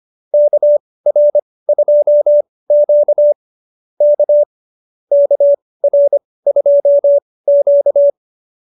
Morse code, heard as a keyed tone, about 25 wpm, spaced as for 18 wpm, 590 Hz. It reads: KR2Q K KR2Q